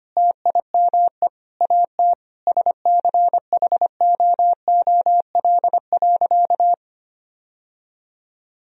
Morse code, heard as a keyed tone, about 25 wpm, 700 Hz. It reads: TIME AT SCHOOL.
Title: Sentences from Top 1000 Words - Review Part 1 - 25wpm